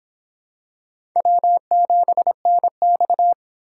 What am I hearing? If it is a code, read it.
Morse code, heard as a keyed tone, about 26 wpm, 700 Hz. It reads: W7NX